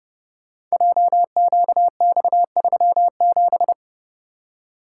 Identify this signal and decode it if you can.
Morse code, heard as a keyed tone, about 30 wpm, 700 Hz. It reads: JQX37
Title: Code Groups - 5 Letter:Numbers - 30wpm